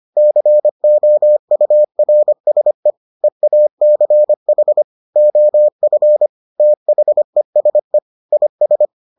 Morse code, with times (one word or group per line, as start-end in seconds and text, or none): 0.2-2.9 COURSE
3.2-4.8 EACH
5.2-6.3 OF
6.6-8.0 THESE
8.3-8.9 IS